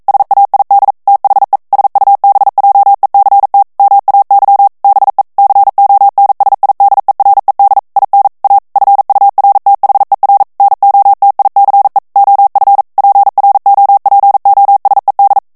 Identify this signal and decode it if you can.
Morse code, heard as a keyed tone, about 42 wpm, 800 Hz. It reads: SAID THE SUBJECT MAY BE CONSIDERED IN A FURTHER NOTICE OF PROPOSED